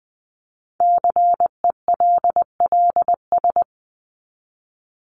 Morse code, heard as a keyed tone, about 20 wpm, 700 Hz. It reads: CELLS
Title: ES:S Suffix - Top 100 Words with Base Words 20wpm